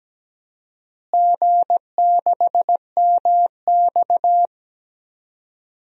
Morse code, heard as a keyed tone, about 17 wpm, 700 Hz. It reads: G6MX
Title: DXCC-100 Call Signs - 4 Characters - Repeated 3x - 17wpm